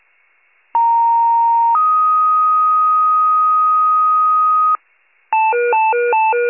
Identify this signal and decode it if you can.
Two-tone page, heard as a one-second tone, then a three-second tone, about 930 then 1250 Hz. It is heard from 0.7 to 4.8 s.